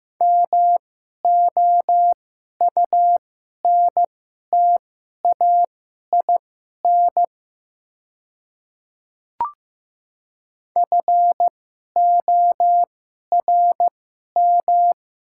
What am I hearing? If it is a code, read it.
Morse code, heard as a keyed tone, about 15 wpm, spaced as for 11 wpm, 700 Hz. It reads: MOUNTAIN FORM
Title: 2x Character Spacing - Top 300 Words - 15wpm